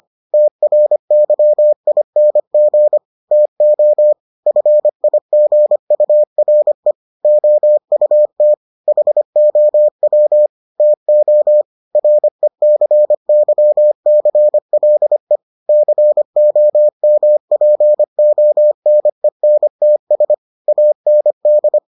Morse code, heard as a keyed tone, about 25 wpm, 600 Hz. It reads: TRYING TO FIGURE OUT HOW TO RECYCLE COMPONENTS AND